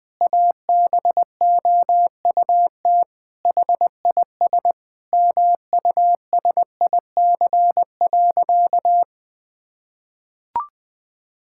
Morse code, heard as a keyed tone, about 20 wpm, 700 Hz. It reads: ABOUT HIS MUSIC.
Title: Sentences from Top 1000 Words - Review Part 6 - 20wpm